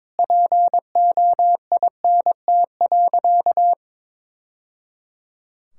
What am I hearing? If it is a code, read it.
Morse code, heard as a keyed tone, about 22 wpm, 700 Hz. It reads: POINT.
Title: Sentences from Top 1000 Words - Review Part 5 - 22wpm